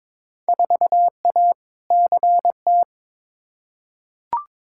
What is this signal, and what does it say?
Morse code, heard as a keyed tone, about 21 wpm, 700 Hz. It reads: 4A CT
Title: Field Day Exchanges Encore - 22wpm